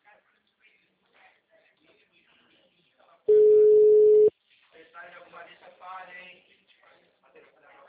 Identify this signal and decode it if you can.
Ringback tone (Europe, 425 Hz), with speech.
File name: out- -2050- - - .52257